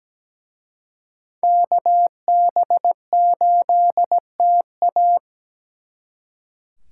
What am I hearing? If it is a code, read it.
Morse code, heard as a keyed tone, about 20 wpm, 700 Hz. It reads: KB8TA
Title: Speed Racing - USA 2x2 Call Signs 17wpm